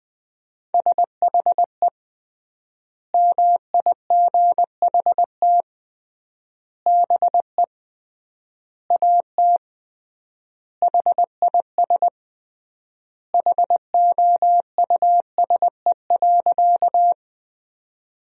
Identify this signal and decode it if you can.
Morse code, heard as a keyed tone, about 20 wpm, 700 Hz. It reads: SHE MIGHT BE AT HIS HOUSE.
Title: 3x Word-Spacing - Sentences from Top 300 Words - 20wpm